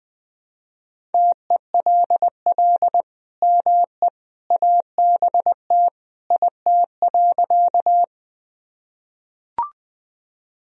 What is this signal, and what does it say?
Morse code, heard as a keyed tone, about 20 wpm, 700 Hz. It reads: TELL ME ABT IT.
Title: Sentences from Top 200 Words and Abbreviations 20wpm